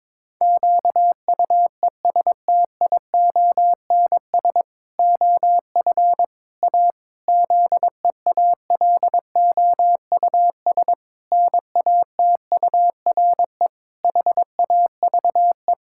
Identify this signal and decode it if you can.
Morse code, heard as a keyed tone, about 22 wpm, 700 Hz. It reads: QUESTIONS OF A ZEALOUS NATURE HAVE
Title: Pangram Challenge v2 - 22wpm